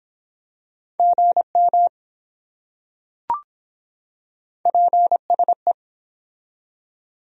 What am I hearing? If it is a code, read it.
Morse code, heard as a keyed tone, about 24 wpm, 700 Hz. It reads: GM PSE